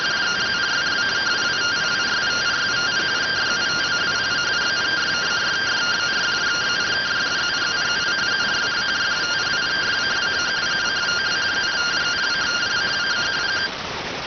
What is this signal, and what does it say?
Two-tone teleprinter signal, about 170 Hz shift.